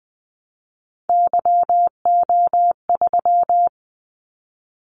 Morse code, heard as a keyed tone, about 20 wpm, 700 Hz. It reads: YO3
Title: Callsign Prefixes for Top 25 DXCC - 20wpm